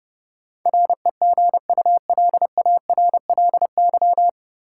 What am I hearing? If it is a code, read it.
Morse code, heard as a keyed tone, about 30 wpm, 700 Hz. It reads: REGULARLY
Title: Sentences from Top 1400 Words - New Word Drill - 30wpm